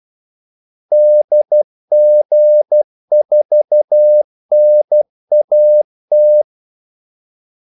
Morse code, heard as a keyed tone, about 12 wpm, 600 Hz. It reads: DG4NAT
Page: 2 ekstra ordmellemrum mellem kaldesignalerne